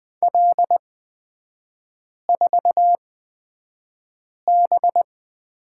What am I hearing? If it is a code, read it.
Morse code, heard as a keyed tone, about 20 wpm, spaced as for 5 wpm, 700 Hz. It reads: L4B
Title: 224 Lesson 36e - BT - Three Characters 20wpm